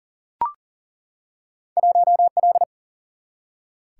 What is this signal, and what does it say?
Morse code, heard as a keyed tone, about 40 wpm, 700 Hz. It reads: 1L
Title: Single-Letter Single-Number 40wpm